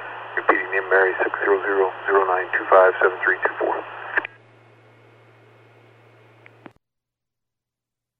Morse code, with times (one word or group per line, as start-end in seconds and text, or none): none